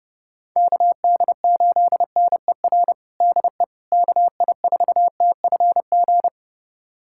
Morse code, heard as a keyed tone, about 30 wpm, 700 Hz. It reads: KD8NER DE KI4TFG
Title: USA Callsign DE Callsign - 6 Characters - 30wpm